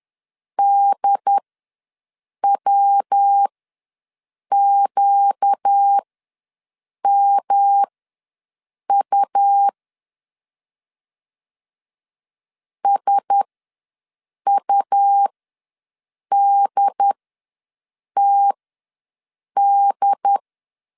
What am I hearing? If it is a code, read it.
Morse code, heard as a keyed tone, about 10 wpm, spaced as for 6 wpm, 800 Hz. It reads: DWQMU SUDTD
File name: LEZIONI / 5 LBVC / 5 L ws 10wpm 3636